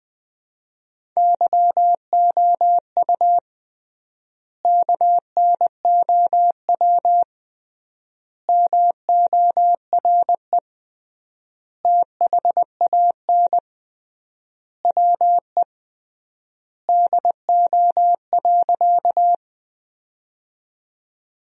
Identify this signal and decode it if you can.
Morse code, heard as a keyed tone, about 21 wpm, 700 Hz. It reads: YOU KNOW MORE THAN WE DO.
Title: 3x Word-Spacing - Sentences from Top 100 Words - 20wpm